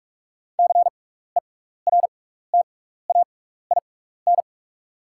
Morse code, heard as a keyed tone, about 44 wpm, spaced as for 15 wpm, 700 Hz. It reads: CERTAIN
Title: 6x Character Spacing - Top 500 Words - 45wpm